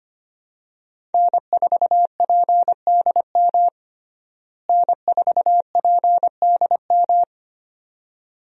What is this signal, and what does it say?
Morse code, heard as a keyed tone, about 25 wpm, 700 Hz. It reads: N4PDM N4PDM
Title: USA Call Signs - 5 Characters - Repeated 2x - 25wpm